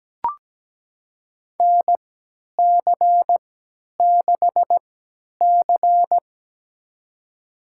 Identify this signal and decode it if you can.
Morse code, heard as a keyed tone, about 17 wpm, spaced as for 10 wpm, 700 Hz. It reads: NC6C
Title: USA Call Signs - 4 Characters - 3x Character Spacing - 17wpm